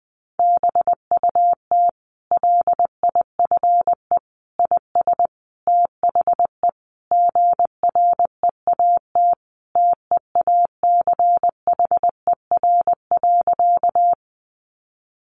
Morse code, heard as a keyed tone, about 20 wpm, 700 Hz. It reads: BUT LIFE IS THE GREAT TEACHER.
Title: Sentences from Top 900 Words 20wpm